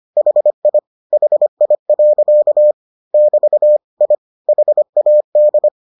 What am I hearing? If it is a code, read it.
Morse code, heard as a keyed tone, about 25 wpm, 600 Hz. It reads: HI HI. <BT> I HAD